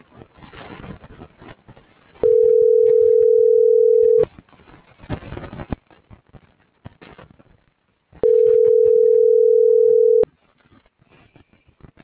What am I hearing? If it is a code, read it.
Ringback tone (North America), 440 + 480 Hz.